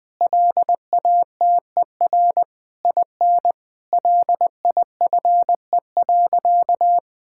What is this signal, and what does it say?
Morse code, heard as a keyed tone, about 20 wpm, 700 Hz. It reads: LATER IN LIFE.
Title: Difficult Sentences with Hints - 20wpm